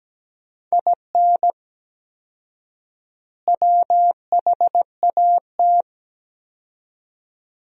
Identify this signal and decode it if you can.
Morse code, heard as a keyed tone, about 17 wpm, 700 Hz. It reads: IN WHAT
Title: 4x Word-Spacing - Sentences from Top 200 Words - 17wpm